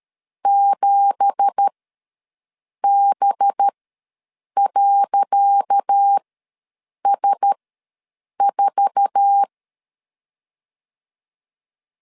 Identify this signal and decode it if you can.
Morse code, heard as a keyed tone, about 13 wpm, spaced as for 7 wpm, 800 Hz. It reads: 7B.S4
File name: LEZIONI / 10 SIGNS / 10C DOT ws 12wpm 3548